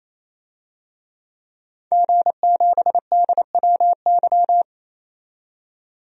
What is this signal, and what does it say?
Morse code, heard as a keyed tone, about 28 wpm, 700 Hz. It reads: G7DWY
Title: DXCC-100 Call Signs - 5 Characters - 28wpm